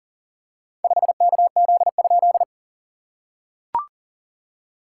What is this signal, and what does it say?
Morse code, heard as a keyed tone, about 40 wpm, 700 Hz. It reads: 5KZ?
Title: SOTA QSO Practice 40wpm